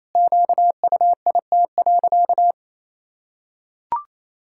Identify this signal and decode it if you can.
Morse code, heard as a keyed tone, about 28 wpm, 700 Hz. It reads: QUIT.
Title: Pangram Challenge v2 - 28wpm